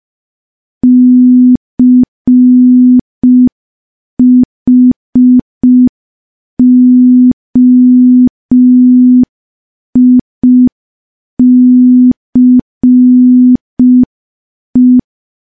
Morse code, heard as a keyed tone, about 5 wpm, 250 Hz. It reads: CHOICE
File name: morse / morse-training-05WPM